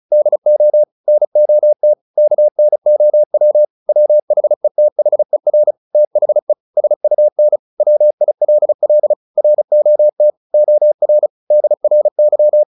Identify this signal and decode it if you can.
Morse code, heard as a keyed tone, about 34 wpm, 600 Hz. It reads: DO NOT KNOW WHETHER THE SUN WILL ROT OR DRY
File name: the old man and the sea / the old man and the sea 1 35wpm